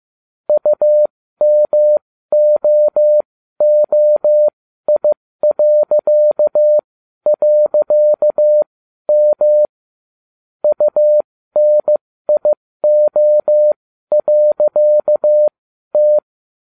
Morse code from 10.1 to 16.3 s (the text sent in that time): UNIO.T